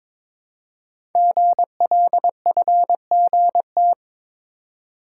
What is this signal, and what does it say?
Morse code, heard as a keyed tone, about 22 wpm, 700 Hz. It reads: GLFGT
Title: Code Groups - 5 Letters - 22wpm